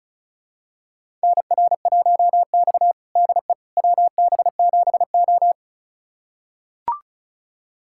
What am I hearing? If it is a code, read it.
Morse code, heard as a keyed tone, about 36 wpm, 700 Hz. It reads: NR1X DE WB7O
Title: USA Callsign DE Callsign - 4 Characters - 35wpm